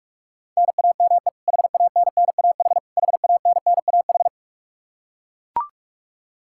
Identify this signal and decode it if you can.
Morse code, heard as a keyed tone, about 42 wpm, 700 Hz. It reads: NAME HANNAH HANNAH
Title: Practice QSO - Exchange 1 - Beginning - 45wpm